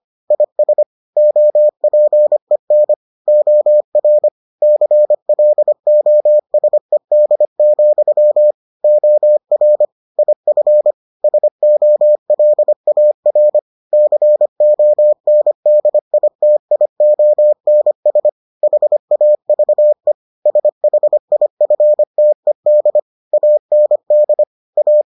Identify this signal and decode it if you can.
Morse code, heard as a keyed tone, about 25 wpm, 600 Hz. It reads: IS OPEN OR CLOSED, OR IF SOLAR CONDITIONS HAVE SHIFTED AND A